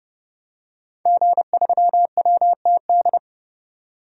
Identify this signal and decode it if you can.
Morse code, heard as a keyed tone, about 30 wpm, 700 Hz. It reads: G3WTD
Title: DXCC-100 Call Signs - 5 Characters - 30wpm